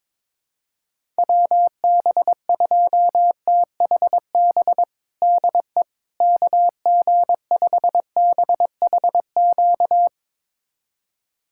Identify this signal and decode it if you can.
Morse code, heard as a keyed tone, about 22 wpm, 700 Hz. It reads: WB2THB DE KG5BHQ
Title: USA Callsign DE Callsign 22wpm